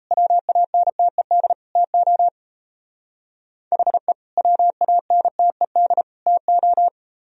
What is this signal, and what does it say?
Morse code, heard as a keyed tone, about 34 wpm, 700 Hz. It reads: WANTED TO HE WANTED TO
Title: Speed-Racing - Sets of 3 Words 28wpm